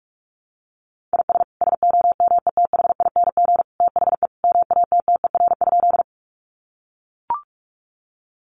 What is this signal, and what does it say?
Morse code, heard as a keyed tone, about 44 wpm, 700 Hz. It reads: IS SOMETHING THE MATTER?
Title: Sentences from Top 700 Words 45wpm